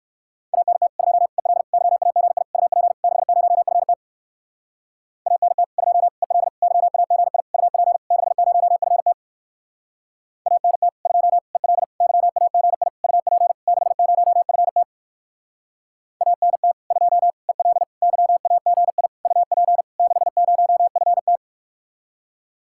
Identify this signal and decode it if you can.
Morse code, heard as a keyed tone, about 54 wpm, 700 Hz. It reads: ANT 2 EL YAGI UP 60FT ANT 2 EL YAGI UP 60FT ANT 2 EL YAGI UP 60FT ANT 2 EL YAGI UP 60FT
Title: Speed-Racing Common QSO Phrases - 45wpm